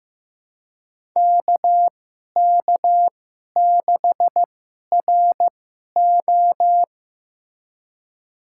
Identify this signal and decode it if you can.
Morse code, heard as a keyed tone, about 15 wpm, spaced as for 11 wpm, 700 Hz. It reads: KK6RO
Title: USA Call Signs - 5 Characters - 2x Character Spacing - 15wpm